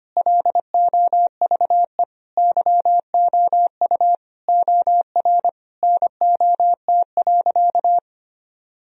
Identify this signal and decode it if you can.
Morse code, heard as a keyed tone, about 25 wpm, 700 Hz. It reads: LOVE YOU OR NOT.
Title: Sentences from Top 1000 Words - Review Part 3 - 25wpm